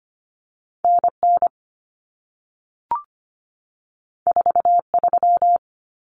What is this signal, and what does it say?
Morse code, heard as a keyed tone, about 25 wpm, 700 Hz. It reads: NN 43
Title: Common Cut Numbers - 2-Digit 25wpm